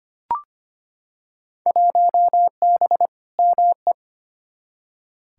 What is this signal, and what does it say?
Morse code, heard as a keyed tone, about 25 wpm, 700 Hz. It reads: 1B ME
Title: Field Day Exchanges - 25wpm